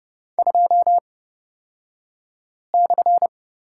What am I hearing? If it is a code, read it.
Morse code, heard as a keyed tone, about 30 wpm, spaced as for 5 wpm, 700 Hz. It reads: 2/
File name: Lesson 38d - Letter q - Two Characters 30wpm